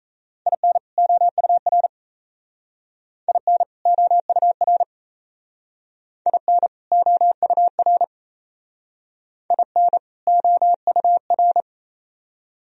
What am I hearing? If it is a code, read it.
Morse code, heard as a keyed tone, about 34 wpm, 700 Hz. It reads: IN OUR IN OUR IN OUR IN OUR
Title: Speed-Racing - Sets of 2 Words 28wpm